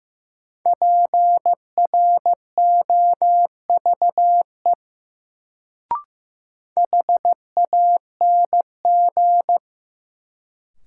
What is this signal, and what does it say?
Morse code, heard as a keyed tone, about 15 wpm, 700 Hz. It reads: PROVE HANG
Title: Sentences from Top 1600 Words - New Word Drill Alt - 15wpm